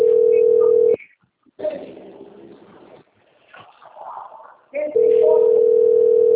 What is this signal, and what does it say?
Ringback tone (North America), 440 + 480 Hz, with speech.
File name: rg-1600-unknown- - - .29809